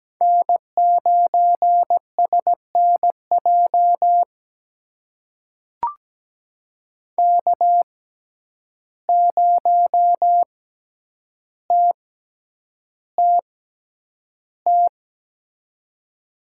Morse code, heard as a keyed tone, about 17 wpm, spaced as for 9 wpm, 700 Hz. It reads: N9SNJ K0TTT
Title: USA Call Signs - 5 Characters - 6x Character Spacing - 17wpm